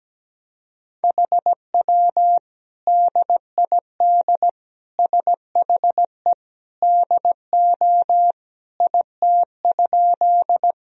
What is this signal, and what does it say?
Morse code, heard as a keyed tone, about 17 wpm, 700 Hz. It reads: HW DID SHE DO IT?
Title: Sentences from Top 100 Words and Abbreviations 17wpm